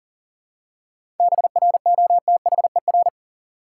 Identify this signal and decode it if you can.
Morse code, heard as a keyed tone, about 40 wpm, 700 Hz. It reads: BROTHER
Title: Seven Letter Words - 40wpm